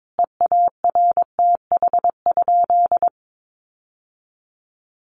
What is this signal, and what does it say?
Morse code, heard as a keyed tone, about 22 wpm, 700 Hz. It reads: EARTH?
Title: Sentences from Top 1200 Words 22wpm